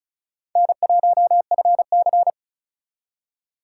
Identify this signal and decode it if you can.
Morse code, heard as a keyed tone, about 44 wpm, 700 Hz. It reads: N1FC